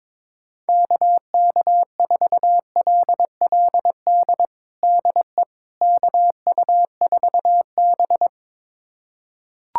Morse code, heard as a keyed tone, about 22 wpm, 700 Hz. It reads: KK4LLD DE KU4B
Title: USA Callsign DE Callsign 22wpm